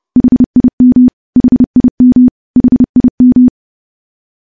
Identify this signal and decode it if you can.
Morse code, heard as a keyed tone, about 30 wpm, 250 Hz. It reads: HIM HIM HIM
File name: morse / morse-training-30WPM